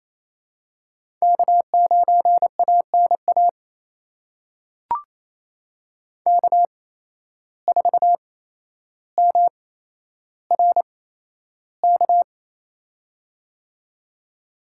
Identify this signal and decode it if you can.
Morse code, heard as a keyed tone, about 27 wpm, spaced as for 12 wpm, 700 Hz. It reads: K9ANA K4MRK
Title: USA Call Signs - 5 Characters - 8x Character Spacing - 28wpm